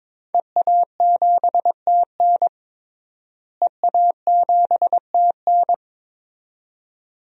Morse code, heard as a keyed tone, about 22 wpm, 700 Hz. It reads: EA7TN EA7TN
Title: DXCC-100 Call Signs - 5 Characters - Repeated 2x - 22wpm